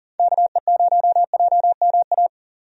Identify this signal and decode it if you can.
Morse code, heard as a keyed tone, about 40 wpm, 690 Hz. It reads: KE0JMA
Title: USA Call Signs - 6 Characters - 40wpm